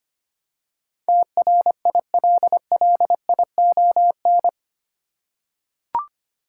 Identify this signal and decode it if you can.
Morse code, heard as a keyed tone, about 25 wpm, 700 Hz. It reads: TRILLION
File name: 8 Letter Words - First 3 Letters Given - 25wpm